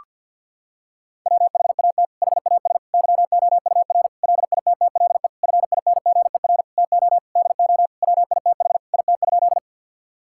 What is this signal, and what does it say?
Morse code, heard as a keyed tone, about 50 wpm, 700 Hz. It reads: WHAT HAS YOUR LITTLE FINGER TO DO WITH IT?
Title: Sentences from Top 1600 Words 50wpm